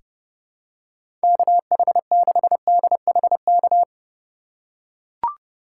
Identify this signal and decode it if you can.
Morse code, heard as a keyed tone, about 29 wpm, 700 Hz. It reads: KH6DHK
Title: K1USN SST QSO Practice - 30wpm